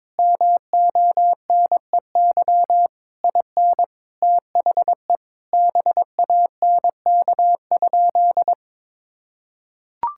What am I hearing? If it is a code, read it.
Morse code, heard as a keyed tone, about 22 wpm, 700 Hz. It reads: MONEY IN THE BANK?
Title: Sentences from Top 1000 Words - Review Part 6 - 22wpm